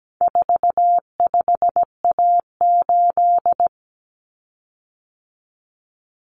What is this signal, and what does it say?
Morse code, heard as a keyed tone, about 17 wpm, 700 Hz. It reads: 45A8
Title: Common Cut Numbers - 4-Digit 17wpm